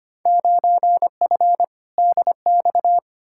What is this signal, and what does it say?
Morse code, heard as a keyed tone, about 25 wpm, 700 Hz. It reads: 9F DX
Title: Field Day Exchanges Encore - 25wpm